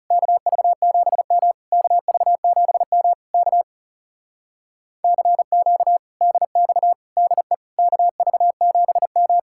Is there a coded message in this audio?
Yes: Morse code from 0.1 to 9.4 s, heard as a keyed tone, about 40 wpm, 700 Hz. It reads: KV7M KV7M K CQ DX DE KV7M